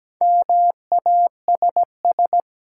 Morse code, heard as a keyed tone, about 17 wpm, 700 Hz. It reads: MASS
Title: Sets of 5 Words Encore - New Word Drill - 17wpm